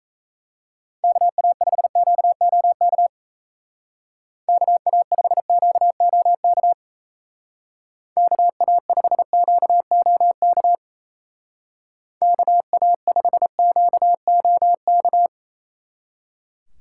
Morse code, heard as a keyed tone, about 35 wpm, 700 Hz. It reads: KA5QOK KA5QOK KA5QOK KA5QOK